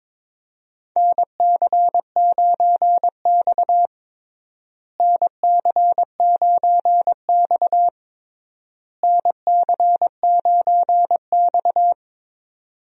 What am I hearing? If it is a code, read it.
Morse code, heard as a keyed tone, about 22 wpm, 700 Hz. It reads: NC9X NC9X NC9X
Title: USA Call Signs - 4 Characters - Repeated 3x - 22wpm